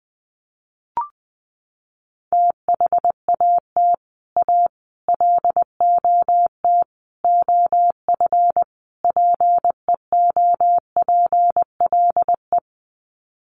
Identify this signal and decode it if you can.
Morse code, heard as a keyed tone, about 20 wpm, 700 Hz. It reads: THAT A LOT OF PEOPLE
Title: Sets of 5 Words 20wpm